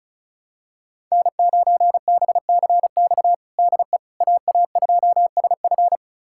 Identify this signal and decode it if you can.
Morse code, heard as a keyed tone, about 36 wpm, 700 Hz. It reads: N9BCX DE AA2SF
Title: USA Callsign DE Callsign - 5 Characters - 35wpm